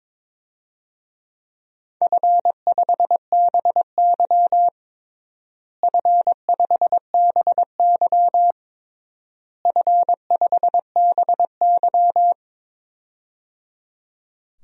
Morse code, heard as a keyed tone, about 22 wpm, 700 Hz. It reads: F5BY F5BY F5BY
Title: DXCC-100 Call Signs - 4 Characters - Repeated 3x - 22wpm